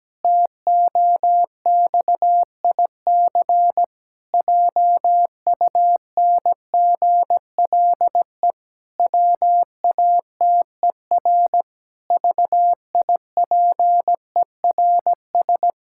Morse code, heard as a keyed tone, about 17 wpm, 700 Hz. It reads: TOXIC JUNGLE WATER VIPERS
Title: Pangram Challenge v2 - 17wpm